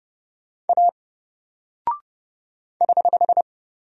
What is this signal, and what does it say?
Morse code, heard as a keyed tone, about 29 wpm, 700 Hz. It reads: A <HH>
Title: Single Character (Letter:Number:Punctuation:Prosign) 30wpm